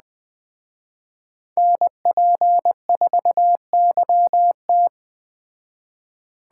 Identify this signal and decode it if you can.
Morse code, heard as a keyed tone, about 20 wpm, 700 Hz. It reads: NP4YT